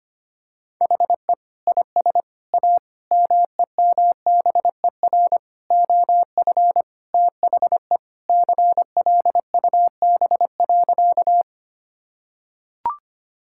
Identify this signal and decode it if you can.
Morse code, heard as a keyed tone, about 25 wpm, 700 Hz. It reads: HE IS A MEMBER OF THE CLUB.
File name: practice / 25wpm / Sentences with 3-Word Phrases - 25wpm